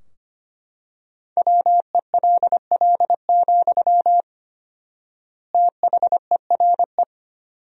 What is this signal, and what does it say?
Morse code, heard as a keyed tone, about 25 wpm, 700 Hz. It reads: WELL, THERE
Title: 4x Word-Spacing - Sentences from Top 500 Words - 25wpm